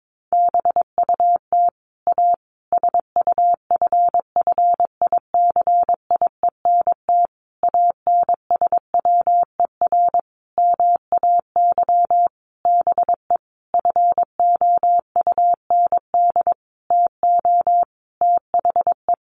Morse code, heard as a keyed tone, about 22 wpm, 700 Hz. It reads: BUT A SUFFICIENT ANSWER MAY BE FOUND TO THE
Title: Sentences from Top 1400 Words 22wpm